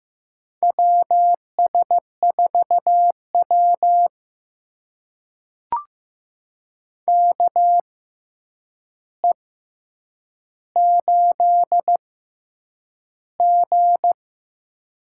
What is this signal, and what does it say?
Morse code, heard as a keyed tone, about 15 wpm, spaced as for 8 wpm, 700 Hz. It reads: WS4W KE8G
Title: USA Call Signs - 4 Characters - 6x Character Spacing - 15wpm